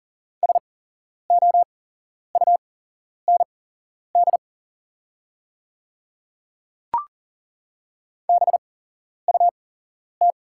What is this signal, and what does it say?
Morse code, heard as a keyed tone, about 40 wpm, spaced as for 11 wpm, 700 Hz. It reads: SOUND BUT
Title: 8x Character Spacing - Top 100 Words - 40wpm